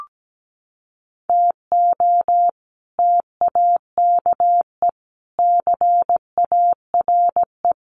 Morse code, heard as a keyed tone, about 17 wpm, 700 Hz. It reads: TO TAKE CARE
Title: Sets of 4 Words 17wpm